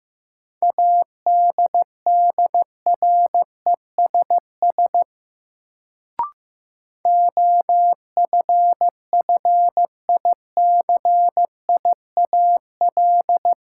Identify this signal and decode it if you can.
Morse code, heard as a keyed tone, about 15 wpm, 700 Hz. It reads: ADDRESS OFFICIAL
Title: Sentences from Top 1000 Words - New Word Drill - 15wpm